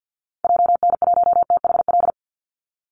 Morse code, heard as a keyed tone, about 50 wpm, 700 Hz. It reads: JN1THL
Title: Call Signs - 50wpm